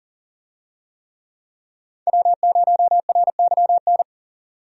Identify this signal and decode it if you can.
Morse code, heard as a keyed tone, about 40 wpm, 700 Hz. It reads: W0RYN
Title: USA Call Signs - 5 Characters - 40wpm